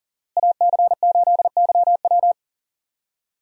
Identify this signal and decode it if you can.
Morse code, heard as a keyed tone, about 38 wpm, 700 Hz. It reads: AC8YW
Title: USA Call Signs - 5 Characters - Repeated 2x - 40wpm